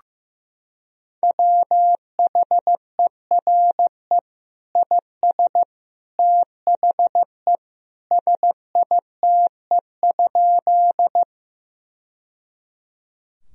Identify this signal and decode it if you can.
Morse code, heard as a keyed tone, about 15 wpm, 700 Hz. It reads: WHERE IS THE SITE?
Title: Sentences from Top 1000 Words - Review Part 4 - 15wpm